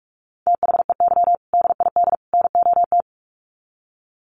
Morse code, heard as a keyed tone, about 44 wpm, 700 Hz. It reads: THEY DID NOT
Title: Sets of 3 Words 45wpm